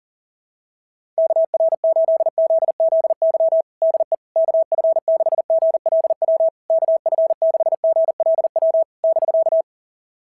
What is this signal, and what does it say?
Morse code, heard as a keyed tone, about 40 wpm, 660 Hz. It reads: KR8ZZY DE KF6GLW KF6GLW <BK>
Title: SOTA QSO Practice 40wpm